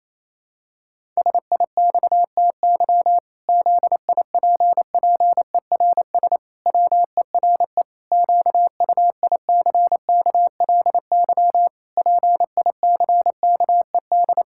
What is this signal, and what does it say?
Morse code, heard as a keyed tone, about 28 wpm, 700 Hz. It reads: SIXTY ZIPPERS WERE QUICKLY PICKED
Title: Pangram Challenge v2 - 28wpm